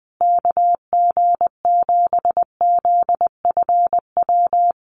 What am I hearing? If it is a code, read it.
Morse code, heard as a keyed tone, about 20 wpm, 700 Hz. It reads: KG7ZFW
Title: USA Call Signs Repeated 2x 20wpm